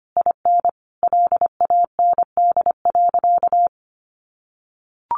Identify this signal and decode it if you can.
Morse code, heard as a keyed tone, about 25 wpm, 700 Hz. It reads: IN LAND.
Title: Sentences from Top 1100 Words 25wpm